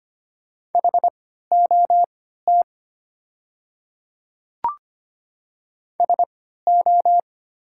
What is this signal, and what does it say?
Morse code, heard as a keyed tone, about 25 wpm, spaced as for 14 wpm, 700 Hz. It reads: HOT SO
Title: 3x Character Spacing - Top 100 Words - 25wpm